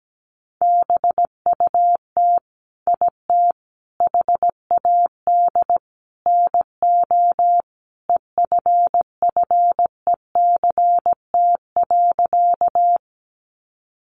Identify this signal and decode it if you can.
Morse code, heard as a keyed tone, about 17 wpm, 700 Hz. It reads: BUT IT HAD NO EFFECT.
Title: Sentences from Top 1000 Words 17wpm